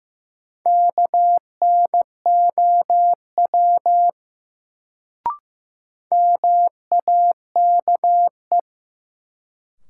Morse code, heard as a keyed tone, about 15 wpm, 700 Hz. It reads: KNOW MAKE
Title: Sentences from Top 100 Words - New Word Drill - 15wpm